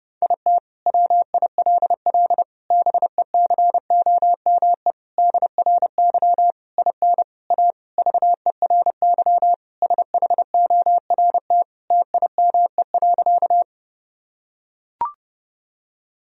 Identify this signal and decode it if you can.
Morse code, heard as a keyed tone, about 32 wpm, 700 Hz. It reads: IT WILL BECOME DRY IN A VERY SHORT TIME.
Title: Sentences from Top 1000 Words - Review Part 3 - 30wpm